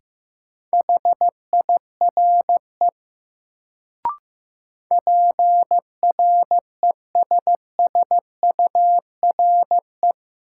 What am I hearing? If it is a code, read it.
Morse code, heard as a keyed tone, about 15 wpm, 700 Hz. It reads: HIRE PRESSURE
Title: Sentences from Top 1400 Words - New Word Drill Alt - 15wpm